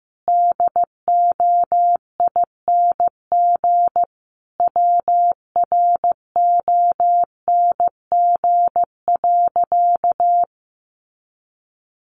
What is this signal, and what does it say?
Morse code, heard as a keyed tone, about 15 wpm, 700 Hz. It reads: DOING WRONG.
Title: Easy Phrases 15wpm